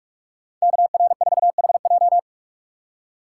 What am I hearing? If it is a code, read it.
Morse code, heard as a keyed tone, about 45 wpm, 700 Hz. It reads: KR4HJ
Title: USA Call Signs - 5 Characters - 2x Character Spacing - 45wpm